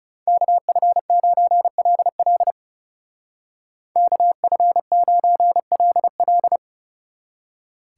Morse code, heard as a keyed tone, about 36 wpm, 700 Hz. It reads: KF9LL KF9LL
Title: Speed Racing - USA 2x2 Call Signs 30wpm